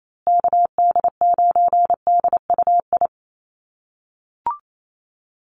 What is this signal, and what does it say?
Morse code, heard as a keyed tone, about 28 wpm, 700 Hz. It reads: KD9DUI
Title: USA Call Signs Repeated 3x 28wpm